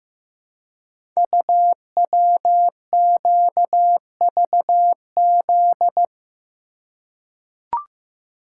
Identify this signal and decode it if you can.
Morse code, heard as a keyed tone, about 15 wpm, 700 Hz. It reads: UWQVZ
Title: Code Groups - 5 Letters - 15wpm